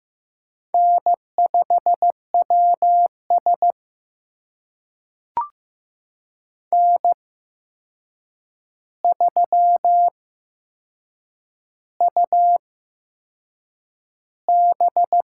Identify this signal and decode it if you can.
Morse code, heard as a keyed tone, about 15 wpm, spaced as for 6 wpm, 700 Hz. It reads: N5WS N3UB
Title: USA Call Signs - 4 Characters - 8x Character Spacing - 15wpm